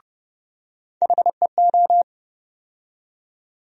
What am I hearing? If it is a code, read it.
Morse code, heard as a keyed tone, about 32 wpm, 700 Hz. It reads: HEO
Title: USA Call Sign Postfixes - 3 Letters 30wpm